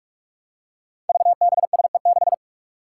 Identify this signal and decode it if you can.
Morse code, heard as a keyed tone, about 46 wpm, 700 Hz. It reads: VBSE6
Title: Five Challenging Letter-Number Combinations 45wpm